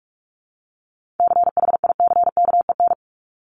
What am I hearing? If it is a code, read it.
Morse code, heard as a keyed tone, about 46 wpm, 700 Hz. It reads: CHICKEN
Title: Multi-letter Phonemes Repeated 3x 45wpm